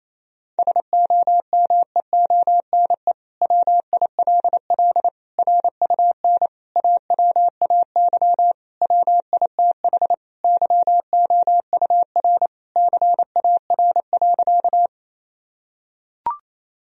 Morse code, heard as a keyed tone, about 27 wpm, 700 Hz. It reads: SOMEONE WILL RUN AWAY WITH YOUR CAR.
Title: Sentences from Top 1000 Words - Review Part 1 - 28wpm